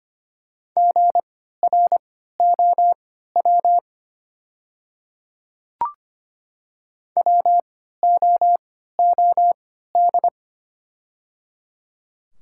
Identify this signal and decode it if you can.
Morse code, heard as a keyed tone, about 25 wpm, spaced as for 14 wpm, 700 Hz. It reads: GROW WOOD
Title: 3x Character Spacing - Top 300 Words - 25wpm